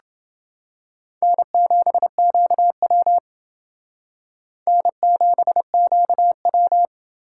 Morse code, heard as a keyed tone, about 28 wpm, 700 Hz. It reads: N7QW N7QW